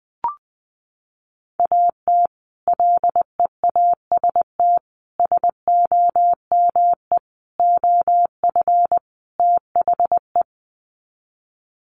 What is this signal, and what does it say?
Morse code, heard as a keyed tone, about 20 wpm, 700 Hz. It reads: AT LEAST SOME OF THE
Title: Sets of 5 Words Encore 20wpm